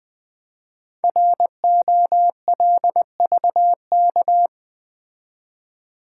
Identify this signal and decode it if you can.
Morse code, heard as a keyed tone, about 20 wpm, 700 Hz. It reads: ROLVK
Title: Code Groups - 5 Letter:Numbers - 20wpm